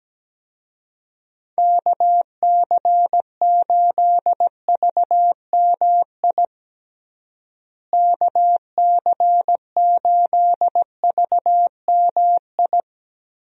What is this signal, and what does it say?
Morse code, heard as a keyed tone, about 17 wpm, 700 Hz. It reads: KC8VMI KC8VMI
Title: USA Call Signs - 6 Characters - Repeated 3x - 17wpm